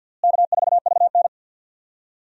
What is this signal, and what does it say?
Morse code, heard as a keyed tone, about 53 wpm, 700 Hz. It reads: K4VN
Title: USA Call Signs - 4 Characters - Repeated 2x - 50wpm